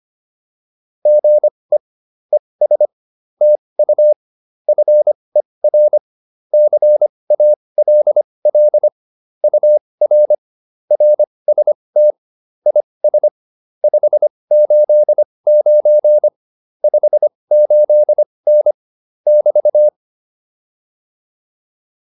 Morse code, heard as a keyed tone, about 25 wpm, spaced as for 20 wpm, 600 Hz. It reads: GE ES TU FER CALL UR RST IS 589 58N <BT>